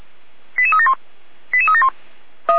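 Five-tone selective call (ZVEI), heard as quick runs of five tones, tones of about 70 ms. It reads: 88371 88371
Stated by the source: ZVEI five-tone selective call 88233, received on an SDR.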